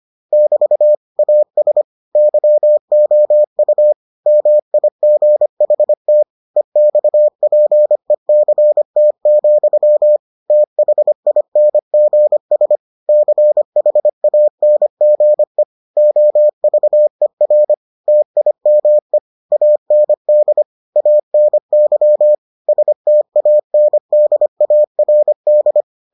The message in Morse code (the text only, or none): <BT> AS YOU MIGHT EXPECT, THINGS CHANGE OVER TIME AND ANY STANDARD